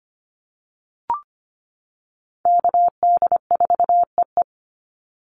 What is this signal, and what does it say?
Morse code, heard as a keyed tone, about 25 wpm, 700 Hz. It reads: KD4EE
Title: Call Signs - 25wpm